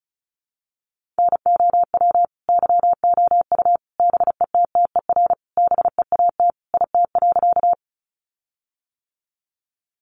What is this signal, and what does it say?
Morse code, heard as a keyed tone, about 34 wpm, 700 Hz. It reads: NOW YOU BETTER BEAT IT.
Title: Sentences from Top 1300 Words 35wpm